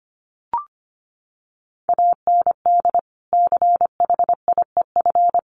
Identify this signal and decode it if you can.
Morse code, heard as a keyed tone, about 25 wpm, 700 Hz. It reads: AND CHIEF
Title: Sets of 5 Words Encore2 25wpm